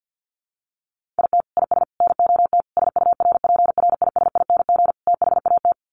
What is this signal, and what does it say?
Morse code, heard as a keyed tone, about 48 wpm, 700 Hz. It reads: IT IS NOT SURPRISING THAT
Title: Sets of 5 Words 50wpm